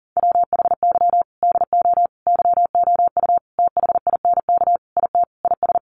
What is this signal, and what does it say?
Morse code, heard as a keyed tone, about 40 wpm, 700 Hz. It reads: WHY DO YOU THINK IT IS
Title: Sentences from Top 1400 Words 40wpm